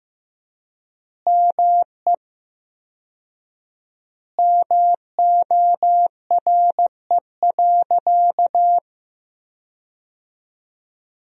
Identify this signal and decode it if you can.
Morse code, heard as a keyed tone, about 16 wpm, 700 Hz. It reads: ME MORE.
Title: 4x Word-Spacing - Sentences from Top 300 Words - 15wpm